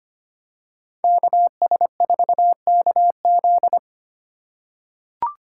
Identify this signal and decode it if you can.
Morse code, heard as a keyed tone, about 25 wpm, 700 Hz. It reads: KS4KZ
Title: USA Call Signs - 4x Character Spacing - 25wpm